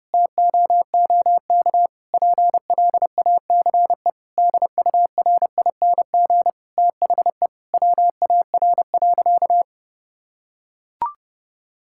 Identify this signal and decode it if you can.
Morse code, heard as a keyed tone, about 32 wpm, 700 Hz. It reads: TOOK PLACE DURING THE WAR.
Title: Sentences from Top 1000 Words - Review Part 6 - 30wpm